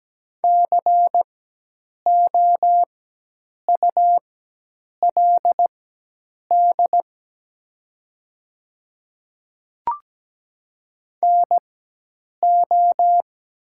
Morse code, heard as a keyed tone, about 17 wpm, spaced as for 8 wpm, 700 Hz. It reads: COULD NO
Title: 4x Character Spacing - Top 100 Words - 17wpm